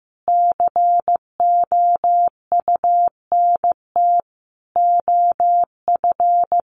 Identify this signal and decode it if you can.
Morse code, heard as a keyed tone, about 15 wpm, 700 Hz. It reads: COUNT OF
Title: Sentences from Top 1500 Words 15wpm